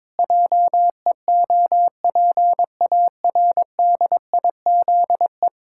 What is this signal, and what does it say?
Morse code, heard as a keyed tone, about 22 wpm, 700 Hz. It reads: JEOPARDIZE
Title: Pangram Challenge v2 - 22wpm